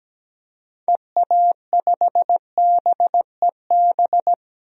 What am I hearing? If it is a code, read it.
Morse code, heard as a keyed tone, about 17 wpm, 700 Hz. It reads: EA5BEB
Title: DXCC-100 Call Signs - 6 Characters - 17wpm